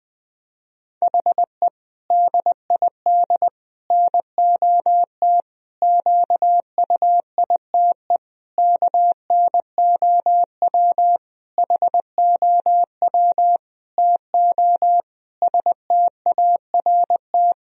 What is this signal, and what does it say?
Morse code, heard as a keyed tone, about 21 wpm, 700 Hz. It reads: HE DID NOT QUITE KNOW HOW TO START
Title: Sentences from Top 1000 Words - Review Part 1 - 20wpm